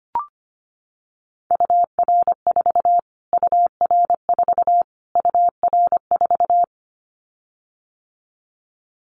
Morse code, heard as a keyed tone, about 25 wpm, 700 Hz. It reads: UR4 UR4 UR4
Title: Callsign Prefixes for Top 25 DXCC - Repeated 3x - 25wpm